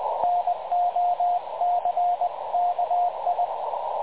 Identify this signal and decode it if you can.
Morse code, heard as a keyed tone, about 20 wpm, 710 Hz, with radio noise. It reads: NOCKI
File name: dlazaawansowanych / nagraniazpasma / websdr recording 3567.0kHz